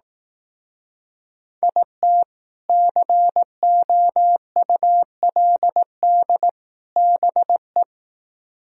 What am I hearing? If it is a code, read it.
Morse code, heard as a keyed tone, about 18 wpm, 700 Hz. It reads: IT COULD BE
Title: Speed-Racing - Sets of 3 Words 15wpm